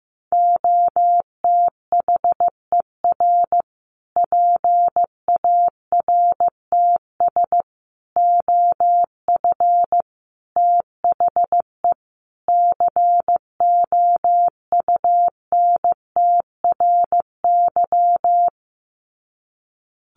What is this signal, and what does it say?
Morse code, heard as a keyed tone, about 15 wpm, 700 Hz. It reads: OTHER PARTS OF THE COUNTRY
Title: Sets of 5 Words Encore 15wpm